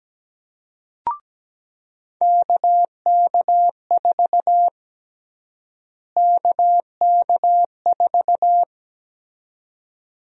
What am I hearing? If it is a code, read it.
Morse code, heard as a keyed tone, about 17 wpm, 700 Hz. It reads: KK4 KK4
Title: USA Call Sign Prefixes - Repeated 2x - 17wpm